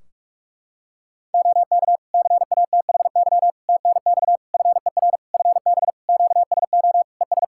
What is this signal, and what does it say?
Morse code, heard as a keyed tone, about 46 wpm, 700 Hz. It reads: OK CATHY TNX FER FB QSO ES
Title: Practice QSO - Exchange 4 - Ending - 45wpm